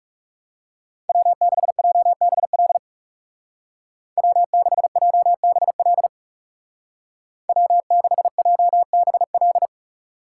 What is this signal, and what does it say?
Morse code, heard as a keyed tone, about 36 wpm, 700 Hz. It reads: W6JBL W6JBL W6JBL
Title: Speed Racing - USA 1x3 Call Signs 30wpm